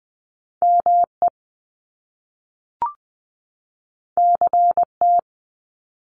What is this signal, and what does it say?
Morse code, heard as a keyed tone, about 20 wpm, 700 Hz. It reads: ME CT
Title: US States 20wpm